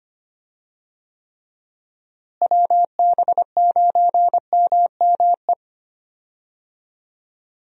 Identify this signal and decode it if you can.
Morse code, heard as a keyed tone, about 25 wpm, 700 Hz. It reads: WB9MME
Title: USA Call Signs - 6 Characters - Repeated 3x - 25wpm